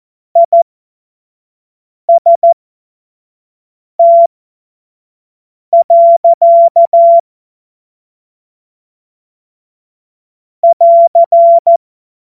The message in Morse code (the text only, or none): IST. <AR>